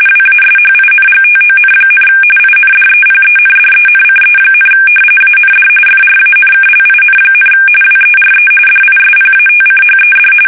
RTTY, about 45.45 baud. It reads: WASHINGTON, INDEPENDENCE, AND FRANKLIN, WERE DISPATCHED
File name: fsk850 / usnavy2